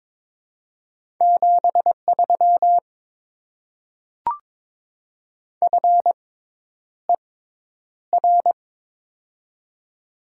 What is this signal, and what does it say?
Morse code, heard as a keyed tone, about 22 wpm, spaced as for 10 wpm, 700 Hz. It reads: 73 FER
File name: Top 20 QSO Elements - 6x Character Spacing - 22wpm